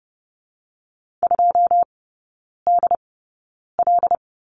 Morse code, heard as a keyed tone, about 31 wpm, spaced as for 9 wpm, 690 Hz. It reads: 2DL
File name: Lesson 16e - Letter c - Three Characters 30wpm